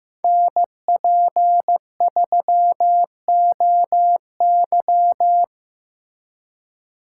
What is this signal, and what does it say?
Morse code, heard as a keyed tone, about 15 wpm, 700 Hz. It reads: NP3OY
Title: USA Call Signs - 3x Character Spacing - 15wpm